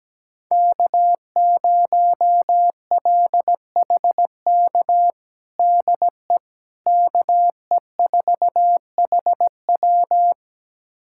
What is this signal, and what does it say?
Morse code, heard as a keyed tone, about 17 wpm, 700 Hz. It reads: K0LHK DE KE4HW
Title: USA Callsign DE Callsign - 5 Characters - 17wpm